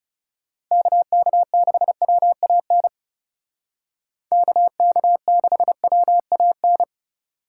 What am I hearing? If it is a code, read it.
Morse code, heard as a keyed tone, about 35 wpm, 700 Hz. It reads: KK6WAN KK6WAN
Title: Speed Racing - USA Call Signs 30wpm